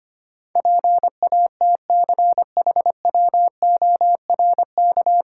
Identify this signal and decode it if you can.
Morse code, heard as a keyed tone, about 25 wpm, 700 Hz. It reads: PATCHWORK
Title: Pangram Challenge v2 - 25wpm